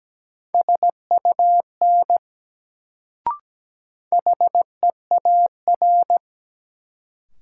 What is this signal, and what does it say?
Morse code, heard as a keyed tone, about 17 wpm, 700 Hz. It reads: SUN HEAR
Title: Sentences from Top 300 Words - New Word Drill - 17wpm